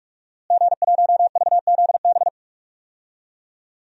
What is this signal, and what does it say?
Morse code, heard as a keyed tone, about 44 wpm, 700 Hz. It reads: G1VZB
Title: DXCC-100 Call Signs - 5 Characters - Repeated 3x - 45wpm